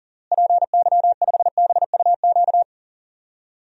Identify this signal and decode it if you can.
Morse code, heard as a keyed tone, about 40 wpm, 700 Hz. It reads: PY5BUQ
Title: DXCC-100 Call Signs - 6 Characters - Repeated 3x - 40wpm